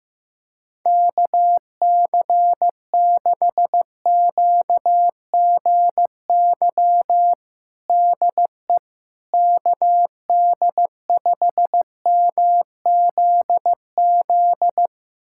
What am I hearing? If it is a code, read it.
Morse code, heard as a keyed tone, about 15 wpm, 700 Hz. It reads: KC6QGY DE KD5MZZ
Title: USA Callsign DE Callsign - 6 Characters - 15wpm